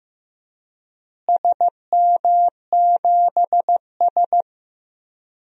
Morse code, heard as a keyed tone, about 15 wpm, 700 Hz. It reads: SM7S
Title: DXCC-100 Call Signs - 4 Characters - Repeated 2x - 15wpm